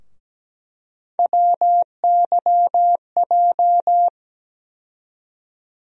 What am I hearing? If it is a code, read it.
Morse code, heard as a keyed tone, about 17 wpm, 700 Hz. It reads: WYJ
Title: USA Call Sign Postfixes - 3 Letters 17wpm